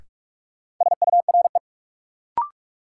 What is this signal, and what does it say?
Morse code, heard as a keyed tone, about 44 wpm, 700 Hz. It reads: SURE
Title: Sentences from Top 300 Words - New Word Drill - 45wpm